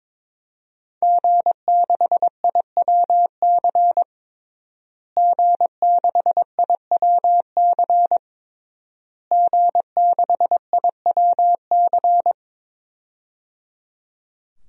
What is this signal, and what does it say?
Morse code, heard as a keyed tone, about 22 wpm, 700 Hz. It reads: G6IWC G6IWC G6IWC
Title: DXCC-100 Call Signs - 5 Characters - Repeated 3x - 22wpm